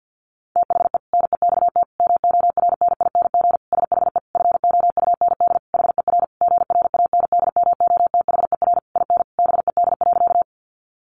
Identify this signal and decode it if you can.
Morse code, heard as a keyed tone, about 50 wpm, 700 Hz. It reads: THE NEXT MORNING SHE FOUND HER GRANDMOTHER IN BED.
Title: Sentences from Top 1200 Words 50wpm